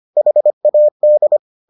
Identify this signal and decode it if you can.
Morse code, heard as a keyed tone, about 25 wpm, 600 Hz. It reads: HAD